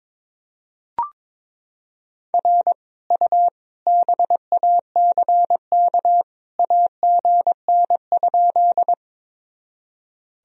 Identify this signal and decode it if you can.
Morse code, heard as a keyed tone, about 22 wpm, 700 Hz. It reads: R U BACK AGN?
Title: Sentences from Top 200 Words and Abbreviations 22wpm